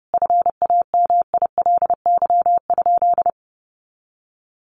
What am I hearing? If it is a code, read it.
Morse code, heard as a keyed tone, about 30 wpm, 700 Hz. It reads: FAMILY?
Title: Sentences from Top 1100 Words 30wpm